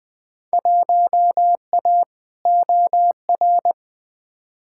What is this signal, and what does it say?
Morse code, heard as a keyed tone, about 20 wpm, 700 Hz. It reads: 1A OR
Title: Field Day Exchanges - 20wpm